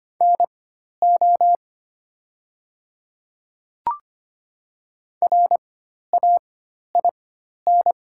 Morse code, heard as a keyed tone, about 25 wpm, spaced as for 12 wpm, 700 Hz. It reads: NO RAIN
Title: 4x Character Spacing - Top 300 Words - 25wpm